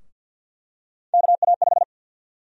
Morse code, heard as a keyed tone, about 47 wpm, 700 Hz. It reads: KA5
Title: USA Call Sign Prefixes - 50wpm